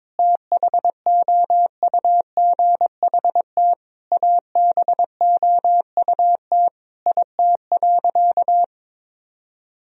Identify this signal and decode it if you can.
Morse code, heard as a keyed tone, about 22 wpm, 700 Hz. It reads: THOUGHT ABOUT IT.
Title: Sentences from Top 1000 Words - Review Part 1 - 22wpm